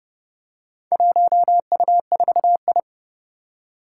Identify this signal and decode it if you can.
Morse code, heard as a keyed tone, about 30 wpm, 700 Hz. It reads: 1U4I
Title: Four Challenging Letter-Number Combinations 30wpm